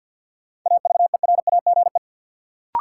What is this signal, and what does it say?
Morse code, heard as a keyed tone, about 53 wpm, 700 Hz. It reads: AVERAGE
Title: Sets of 5 Words - New Word Drill - 50wpm